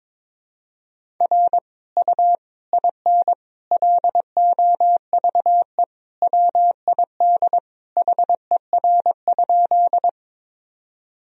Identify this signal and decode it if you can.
Morse code, heard as a keyed tone, about 22 wpm, 700 Hz. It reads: R U IN LOVE WID HER?
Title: Sentences from Top 400 Words and Abbreviations 22wpm